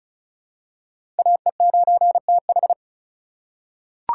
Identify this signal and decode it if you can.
Morse code, heard as a keyed tone, about 36 wpm, 700 Hz. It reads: AE9TH
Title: USA Call Signs - 5 Characters - 2x Character Spacing - 35wpm